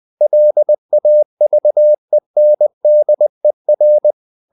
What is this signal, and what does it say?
Morse code, heard as a keyed tone, about 20 wpm, 600 Hz. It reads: LAVENDER
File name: the old man and the sea part 2 20wpm